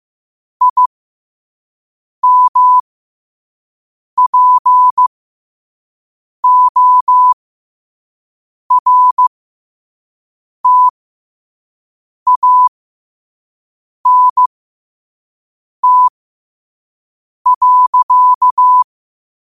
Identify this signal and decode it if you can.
Morse code, heard as a keyed tone, about 15 wpm, spaced as for 5 wpm, 1000 Hz. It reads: IMPORTANT.
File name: qotdfiles / 5wpm